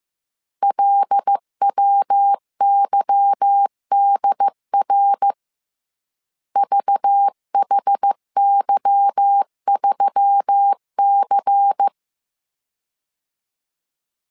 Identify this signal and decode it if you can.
Morse code, heard as a keyed tone, about 15 wpm, 800 Hz. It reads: LWYDR VHY3C
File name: CW / DET / 1 9 1500C S C 14wpm 3245